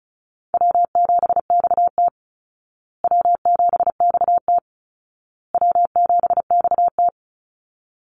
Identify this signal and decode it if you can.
Morse code, heard as a keyed tone, about 34 wpm, 700 Hz. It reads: W7XT W7XT W7XT
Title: Easy Call Signs Repeated 3x - 35wpm